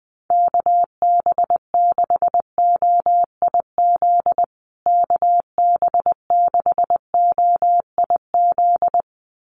Morse code, heard as a keyed tone, about 20 wpm, 700 Hz. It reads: KB6OIZ KB6OIZ
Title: USA Call Signs Repeated 2x 20wpm